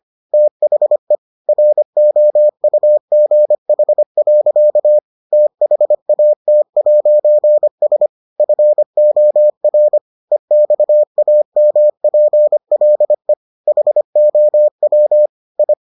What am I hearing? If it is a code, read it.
Morse code, heard as a keyed tone, about 25 wpm, 600 Hz. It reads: THE ROUGH. THAT'S FOR EXAMPLE HOW I